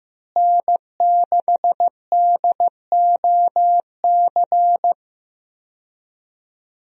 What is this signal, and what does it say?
Morse code, heard as a keyed tone, about 15 wpm, 700 Hz. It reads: N6DOC
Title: USA Call Signs - 5 Characters - Repeated 2x - 15wpm